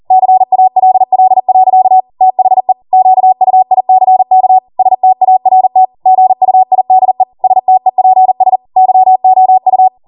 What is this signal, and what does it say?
Morse code, heard as a keyed tone, about 40 wpm, 750 Hz. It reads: CALL. THE QUICK START GUIDE STEPS YOU